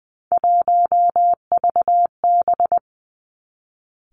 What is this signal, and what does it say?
Morse code, heard as a keyed tone, about 19 wpm, 700 Hz. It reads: 1VB
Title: Three Challenging Letter-Number Combinations 20wpm